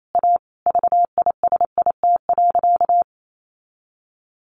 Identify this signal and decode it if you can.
Morse code, heard as a keyed tone, about 28 wpm, 700 Hz. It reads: A VISIT.
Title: Sentences from Top 1400 Words 28wpm